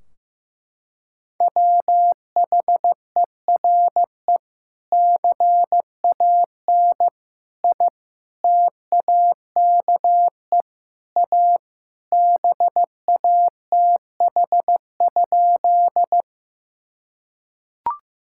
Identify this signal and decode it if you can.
Morse code, heard as a keyed tone, about 16 wpm, 700 Hz. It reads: WHERE CAN I TAKE A BATH?
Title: Sentences from Top 1600 Words 15wpm